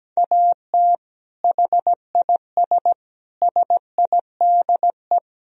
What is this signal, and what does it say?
Morse code, heard as a keyed tone, about 17 wpm, 700 Hz. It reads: AT HIS SIDE
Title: Sentences from Top 1000 Words - Review Part 2 - 17wpm